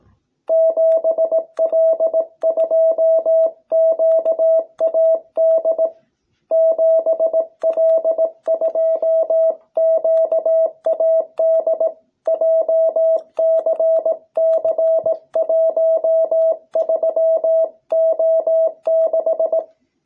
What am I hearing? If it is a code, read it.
Morse code, heard as a keyed tone, about 18 wpm, 660 Hz. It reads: 7L2QAD 7L2QAD JCC13O6